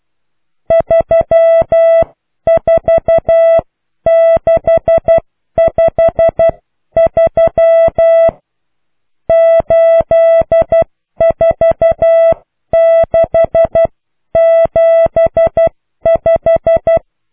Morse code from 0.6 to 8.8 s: 34653